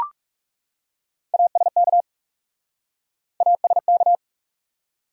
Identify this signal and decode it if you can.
Morse code, heard as a keyed tone, about 44 wpm, 700 Hz. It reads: ASK ASK